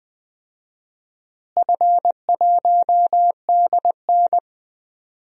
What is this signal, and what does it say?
Morse code, heard as a keyed tone, about 20 wpm, 700 Hz. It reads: F1DN
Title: DXCC-100 Call Signs - 4 Characters - Repeated 3x - 20wpm